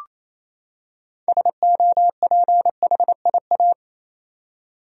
Morse code, heard as a keyed tone, about 27 wpm, 700 Hz. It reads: SOPHIA
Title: USA Common Names - 28wpm